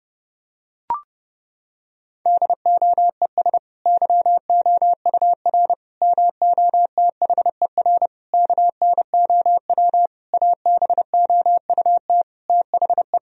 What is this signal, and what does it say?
Morse code, heard as a keyed tone, about 30 wpm, 700 Hz. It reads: DOES YOUR MOTHER KNOW ABOUT THE
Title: Sentences from Top 1000 Words - Review Part 4 - 30wpm